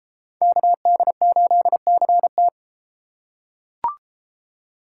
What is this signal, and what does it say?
Morse code, heard as a keyed tone, about 33 wpm, 700 Hz. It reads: KD8CT